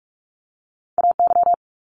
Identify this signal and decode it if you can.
Morse code, heard as a keyed tone, about 46 wpm, 700 Hz. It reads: AY
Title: Multi-letter Phonemes 45wpm